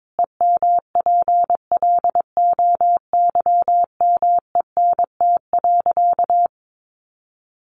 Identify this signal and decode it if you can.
Morse code, heard as a keyed tone, about 22 wpm, 700 Hz. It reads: EMPLOYMENT.
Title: Sentences from Top 1500 Words 22wpm